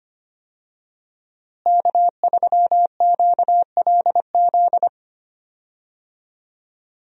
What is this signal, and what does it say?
Morse code, heard as a keyed tone, about 25 wpm, 700 Hz. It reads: K3QLZ
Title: USA Call Signs - 5 Characters - Repeated 2x - 25wpm